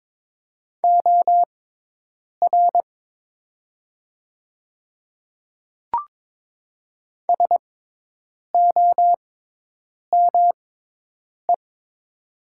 Morse code, heard as a keyed tone, about 22 wpm, spaced as for 8 wpm, 700 Hz. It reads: OR SOME
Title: 6x Character Spacing - Top 200 Words - 22wpm